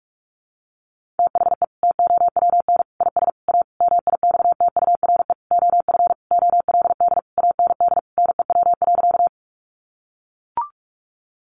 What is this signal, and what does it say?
Morse code, heard as a keyed tone, about 44 wpm, 700 Hz. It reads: THE TOWN IS A MIXTURE OF OLD AND NEW.
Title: Sentences from Top 1500 Words 45wpm